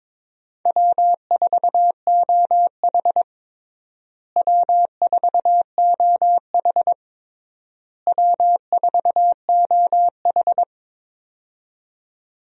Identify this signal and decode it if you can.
Morse code, heard as a keyed tone, about 21 wpm, 700 Hz. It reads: W4OH W4OH W4OH
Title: USA Call Signs - 4 Characters - Repeated 3x - 22wpm